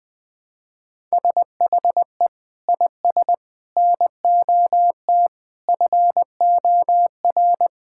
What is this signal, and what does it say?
Morse code, heard as a keyed tone, about 20 wpm, 700 Hz. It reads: SHE IS NOT FOR
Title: Sentences from Top 1000 Words - Review Part 5 - 20wpm